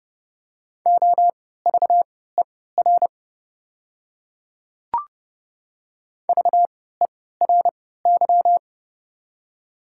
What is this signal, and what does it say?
Morse code, heard as a keyed tone, about 31 wpm, spaced as for 17 wpm, 700 Hz. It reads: OVER VERY
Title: 3x Character Spacing - Top 200 Words - 30wpm